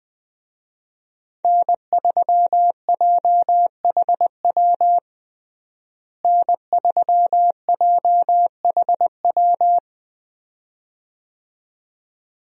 Morse code, heard as a keyed tone, about 20 wpm, 700 Hz. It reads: N3JHW N3JHW
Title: USA Call Signs - 5 Characters - Repeated 2x - 20wpm